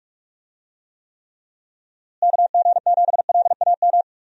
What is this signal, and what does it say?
Morse code, heard as a keyed tone, about 46 wpm, 700 Hz. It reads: KG7LAM
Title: USA Call Signs - 6 Characters - Repeated 3x - 45wpm